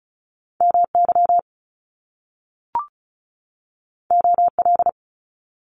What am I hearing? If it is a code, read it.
Morse code, heard as a keyed tone, about 36 wpm, 700 Hz. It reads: MY OL
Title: Top Two Letter Combinations in Words 35wpm